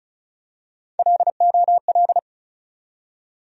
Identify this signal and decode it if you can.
Morse code, heard as a keyed tone, about 36 wpm, 700 Hz. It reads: LOL
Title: USA Call Sign Postfixes - 3 Letters 35wpm